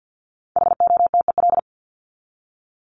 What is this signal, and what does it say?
Morse code, heard as a keyed tone, about 50 wpm, 700 Hz. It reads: HOTEL
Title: Top 1501-1750 Words 50wpm